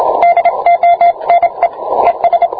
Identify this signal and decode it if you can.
Morse code, heard as a keyed tone, about 26 wpm, 690 Hz, with radio noise. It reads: DONE ES